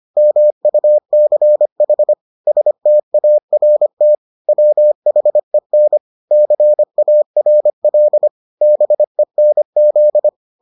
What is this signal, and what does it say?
Morse code, heard as a keyed tone, about 25 wpm, 600 Hz. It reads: MUCH START WHEN CARL BENZ